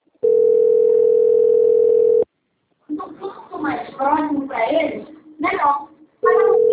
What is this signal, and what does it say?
Ringback tone (North America), 440 + 480 Hz, with speech.
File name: out- - - - - .11639